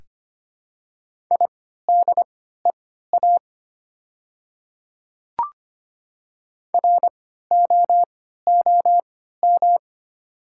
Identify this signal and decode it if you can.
Morse code, heard as a keyed tone, about 25 wpm, spaced as for 14 wpm, 700 Hz. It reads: IDEA ROOM
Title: 3x Character Spacing - Top 300 Words - 25wpm